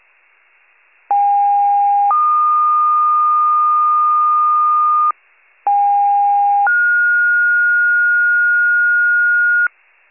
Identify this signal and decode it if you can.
Two-tone page, heard as a one-second tone, then a three-second tone, about 810 then 1220 Hz, about 810 then 1430 Hz, with noise. It is heard from 1.1 to 9.7 s.